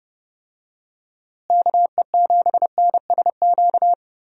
Morse code, heard as a keyed tone, about 32 wpm, 700 Hz. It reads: KE7NSQ
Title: USA Call Signs - 6 Characters - Repeated 3x - 30wpm